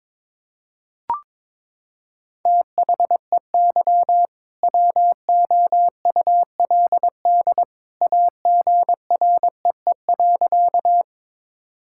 Morse code, heard as a keyed tone, about 22 wpm, 700 Hz. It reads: THEY WOULD AGREE.
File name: practice / 22wpm / Sentences with 2-Word Phrases - 22wpm